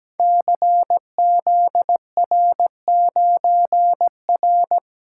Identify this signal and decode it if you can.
Morse code, heard as a keyed tone, about 17 wpm, 690 Hz. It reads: CZR9R
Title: Code Groups - 5 Letter:Numbers - 17wpm